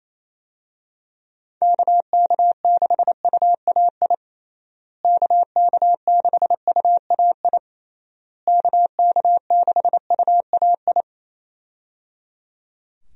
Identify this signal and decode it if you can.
Morse code, heard as a keyed tone, about 28 wpm, 700 Hz. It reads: KK6UAI KK6UAI KK6UAI
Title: USA Call Signs - 6 Characters - Repeated 3x - 28wpm